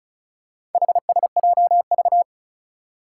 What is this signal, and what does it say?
Morse code, heard as a keyed tone, about 36 wpm, 700 Hz. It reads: HSJV
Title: Four Challenging Letter-Number Combinations 35wpm